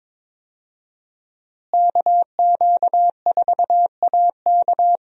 Morse code, heard as a keyed tone, about 22 wpm, 700 Hz. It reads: KQ4AK
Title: USA Call Signs - 5 Characters - Repeated 2x - 22wpm